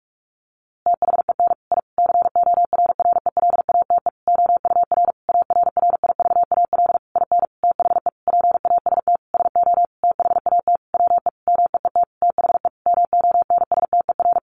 Morse code, heard as a keyed tone, about 44 wpm, 700 Hz. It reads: THEN I CORRELATE OUR ARRIVAL IN THE PAST SO THAT WE MEET THE MONSTER
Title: A Sound Of Thunder 45wpm